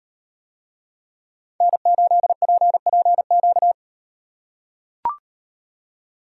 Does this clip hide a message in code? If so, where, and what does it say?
Yes: Morse code from 1.6 to 3.7 s, heard as a keyed tone, about 35 wpm, 700 Hz. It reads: N8PPQ